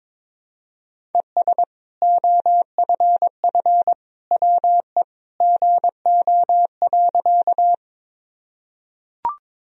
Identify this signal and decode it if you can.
Morse code, heard as a keyed tone, about 22 wpm, 700 Hz. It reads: ES OFF WE GO.
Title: Sentences from Top 200 Words and Abbreviations 22wpm